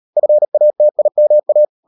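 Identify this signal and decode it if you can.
Morse code, heard as a keyed tone, about 38 wpm, 600 Hz. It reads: FATIMA
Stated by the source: Morse code practice at 38 WPM, sending Ken Follett's novel Never in English, from CW med Gnister.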